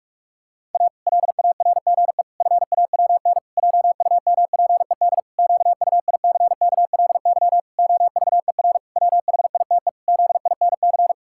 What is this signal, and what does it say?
Morse code, heard as a keyed tone, about 46 wpm, 700 Hz. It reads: A LARGE FAWN JUMPED QUICKLY OVER WHITE ZINC
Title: Pangram Challenge v2 - 45wpm